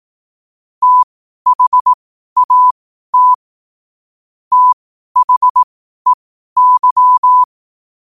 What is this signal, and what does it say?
Morse code, heard as a keyed tone, about 18 wpm, spaced as for 12 wpm, 1000 Hz. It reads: THAT THEY
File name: qotdfiles / 12wpm